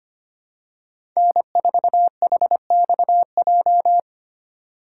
Morse code, heard as a keyed tone, about 25 wpm, 700 Hz. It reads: N4HXJ
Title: USA Call Signs - 5 Characters - 2x Character Spacing - 25wpm